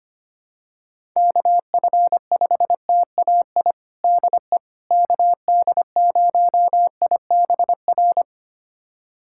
Morse code, heard as a keyed tone, about 25 wpm, 700 Hz. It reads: KF5TAI DE KD0IBR
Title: USA Callsign DE Callsign 25wpm